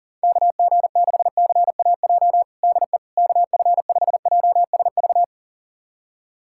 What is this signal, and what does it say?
Morse code, heard as a keyed tone, about 40 wpm, 700 Hz. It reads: KG6CAJ DE KF5JSV
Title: USA Callsign DE Callsign - 6 Characters - 40wpm